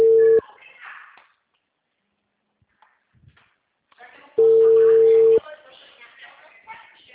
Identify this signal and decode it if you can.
Ringback tone (China, 450 Hz), with speech.
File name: out- -2016- - - .36566